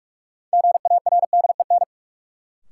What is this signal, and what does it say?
Morse code, heard as a keyed tone, about 44 wpm, 700 Hz. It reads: GARDEN
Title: Sentences from Top 700 Words - New Word Drill - 45wpm